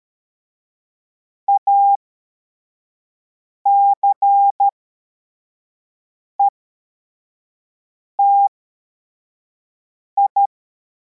Morse code, heard as a keyed tone, about 13 wpm, spaced as for 4 wpm, 800 Hz. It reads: ACETI
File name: formation / cw / Text33